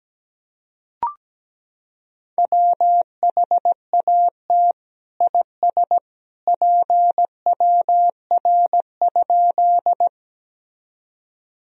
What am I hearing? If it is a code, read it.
Morse code, heard as a keyed tone, about 17 wpm, 700 Hz. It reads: WHAT IS PWR?
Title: Sentences from Top 500 Words and Abbreviations 17wpm